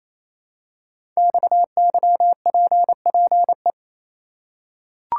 Morse code, heard as a keyed tone, about 28 wpm, 700 Hz. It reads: XYPPE
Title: Code Groups - 5 Letters - 28wpm